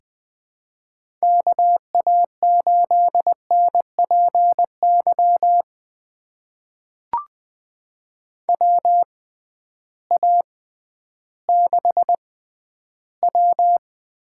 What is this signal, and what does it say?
Morse code, heard as a keyed tone, about 20 wpm, spaced as for 12 wpm, 700 Hz. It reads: KA8NPY WA6W
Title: USA Call Signs - 6 Characters - 6x Character Spacing - 20wpm